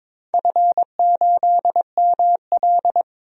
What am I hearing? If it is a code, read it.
Morse code, heard as a keyed tone, about 22 wpm, 700 Hz. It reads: F8ML
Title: DXCC-100 Call Signs - 4 Characters - 22wpm